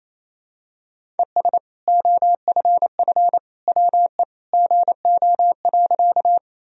Morse code, heard as a keyed tone, about 28 wpm, 700 Hz. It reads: ES OFF WE GO.
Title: Sentences from Top 200 Words and Abbreviations 28wpm